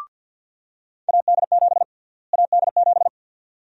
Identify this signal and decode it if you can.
Morse code, heard as a keyed tone, about 49 wpm, 700 Hz. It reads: AD7 AD7
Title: USA Call Sign Prefixes - Repeated 2x - 50wpm